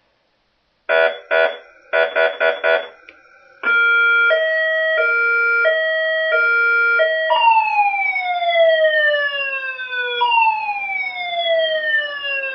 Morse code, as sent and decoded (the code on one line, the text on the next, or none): none